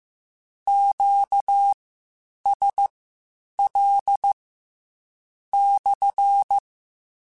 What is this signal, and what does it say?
Morse code, heard as a keyed tone, about 15 wpm, spaced as for 7 wpm, 800 Hz. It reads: QSL/
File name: Q Calls / Q Calls-mp3s / Q CALLS SPSL1 @13